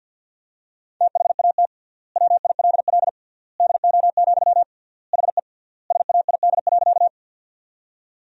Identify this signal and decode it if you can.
Morse code, heard as a keyed tone, about 46 wpm, 700 Hz. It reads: THAT WILL DO, HE SAID.
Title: 3x Word-Spacing - Sentences from Top 100 Words - 50wpm